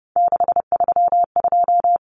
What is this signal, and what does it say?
Morse code, heard as a keyed tone, about 30 wpm, 700 Hz. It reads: 632
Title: Numbers - 3-Digit 30wpm